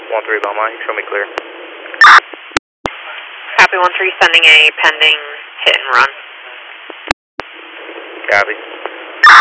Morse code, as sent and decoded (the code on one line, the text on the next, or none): none